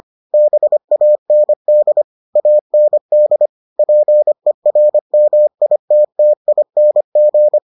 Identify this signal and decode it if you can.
Morse code, heard as a keyed tone, about 25 wpm, 600 Hz. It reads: BAND AND PERMITTING